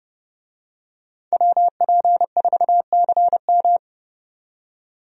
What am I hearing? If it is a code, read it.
Morse code, heard as a keyed tone, about 30 wpm, 700 Hz. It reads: WP4CM
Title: USA Call Signs - 5 Characters - 30wpm